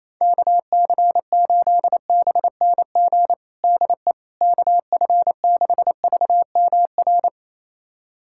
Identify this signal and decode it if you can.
Morse code, heard as a keyed tone, about 28 wpm, 700 Hz. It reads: KC8BNG DE KF6VMR
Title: USA Callsign DE Callsign - 6 Characters - 28wpm